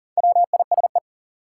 Morse code, heard as a keyed tone, about 40 wpm, 700 Hz. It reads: WISE
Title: Sentences from Top 1600 Words - New Word Drill Alt - 40wpm